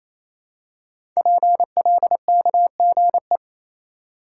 Morse code, heard as a keyed tone, about 27 wpm, 700 Hz. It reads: PLKGE
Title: Code Groups - 5 Letters - 28wpm